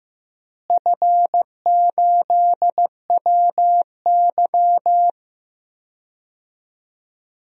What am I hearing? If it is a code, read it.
Morse code, heard as a keyed tone, about 16 wpm, 700 Hz. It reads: F8WY
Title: DXCC-100 Call Signs - 4 Characters - Repeated 3x - 15wpm